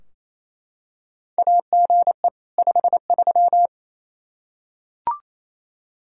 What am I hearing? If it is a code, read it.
Morse code, heard as a keyed tone, about 28 wpm, 700 Hz. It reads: AGE 53
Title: Practice QSO - Exchange 3 - More Info - 28wpm